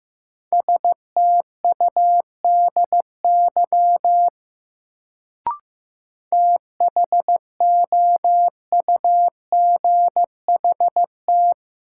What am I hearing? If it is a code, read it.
Morse code, heard as a keyed tone, about 15 wpm, 700 Hz. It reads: STUDY THOUGHT
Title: Sentences from Top 300 Words - New Word Drill - 15wpm